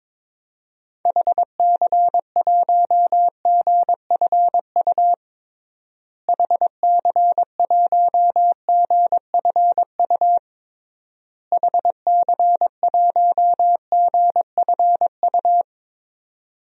Morse code, heard as a keyed tone, about 22 wpm, 700 Hz. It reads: HC1GFU HC1GFU HC1GFU
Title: DXCC-100 Call Signs - 6 Characters - Repeated 3x - 22wpm